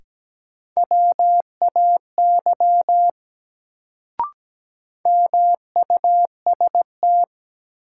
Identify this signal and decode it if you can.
Morse code, heard as a keyed tone, about 17 wpm, 700 Hz. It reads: WAY MUST
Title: Sets of 2 Words - New Word Drill - 17wpm